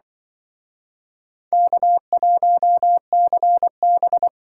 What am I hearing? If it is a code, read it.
Morse code, heard as a keyed tone, about 25 wpm, 700 Hz. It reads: K1CB
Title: Speed Racing - USA 1x2 Call Signs 20wpm